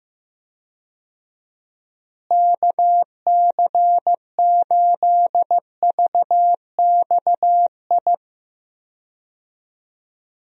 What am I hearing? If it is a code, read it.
Morse code, heard as a keyed tone, about 15 wpm, 700 Hz. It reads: KC8VXI
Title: USA Call Signs - 6 Characters - Repeated 3x - 15wpm